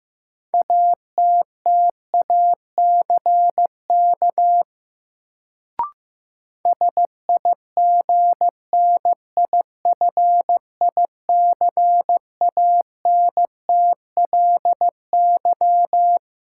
Morse code, heard as a keyed tone, about 15 wpm, 700 Hz. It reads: ATTACK SIGNIFICANTLY
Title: Sentences from Top 1500 Words - New Word Drill - 15wpm